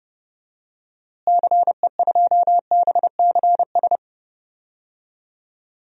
Morse code, heard as a keyed tone, about 30 wpm, 700 Hz. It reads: CE2BCS
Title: DXCC-100 Call Signs - 6 Characters - Repeated 2x - 30wpm